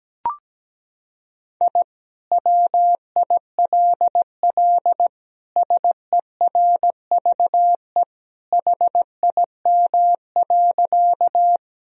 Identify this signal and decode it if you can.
Morse code, heard as a keyed tone, about 17 wpm, 700 Hz. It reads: I WILL SERVE HIM.
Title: Sentences from Top 1000 Words - Review Part 2 - 17wpm